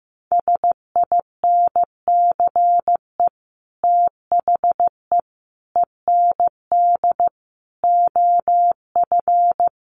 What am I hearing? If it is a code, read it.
Morse code, heard as a keyed tone, about 15 wpm, 700 Hz. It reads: SINCE THE END OF
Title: Sets of 5 Words Encore 15wpm